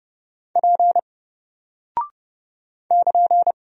Morse code, heard as a keyed tone, about 30 wpm, 700 Hz. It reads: P <KN>
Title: Single Character (Letter:Number:Punctuation:Prosign) Alt 30wpm